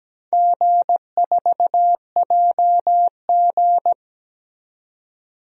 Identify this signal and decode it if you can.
Morse code, heard as a keyed tone, about 17 wpm, 700 Hz. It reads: G4JG
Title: DXCC-100 Call Signs - 4 Characters - 17wpm